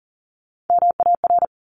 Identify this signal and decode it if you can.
Morse code, heard as a keyed tone, about 42 wpm, 700 Hz. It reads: MAR
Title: ARRL Sections 40wpm